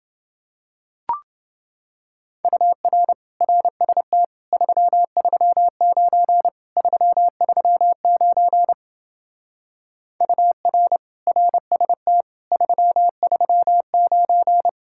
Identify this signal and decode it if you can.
Morse code, heard as a keyed tone, about 29 wpm, 700 Hz. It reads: UR RST 339 339 UR RST 339